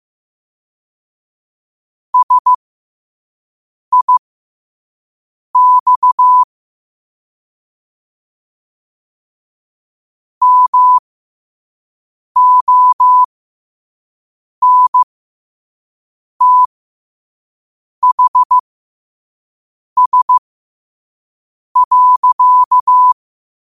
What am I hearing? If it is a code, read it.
Morse code, heard as a keyed tone, about 15 wpm, spaced as for 5 wpm, 1000 Hz. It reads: SIX MONTHS.
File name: qotdfiles / 5wpm